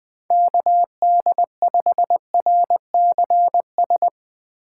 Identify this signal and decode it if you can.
Morse code, heard as a keyed tone, about 22 wpm, 700 Hz. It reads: KD5RCS
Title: Speed Racing - USA 2x3 Call Signs 20wpm